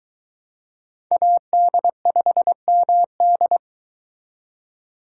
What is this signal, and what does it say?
Morse code, heard as a keyed tone, about 23 wpm, 700 Hz. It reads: AD5MD